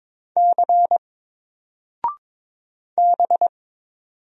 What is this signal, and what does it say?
Morse code, heard as a keyed tone, about 22 wpm, 700 Hz. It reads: C B
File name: Single-Letter-Number-22wpm